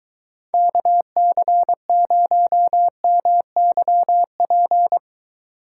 Morse code, heard as a keyed tone, about 21 wpm, 700 Hz. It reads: KC0MYP